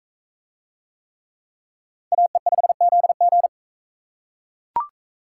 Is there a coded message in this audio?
Yes: Morse code from 2.1 to 3.5 s, heard as a keyed tone, about 40 wpm, 700 Hz. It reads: AE5ZG